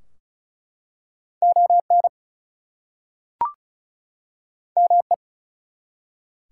Morse code, heard as a keyed tone, about 34 wpm, 700 Hz. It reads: ON ME
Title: Two Letter Words - 35wpm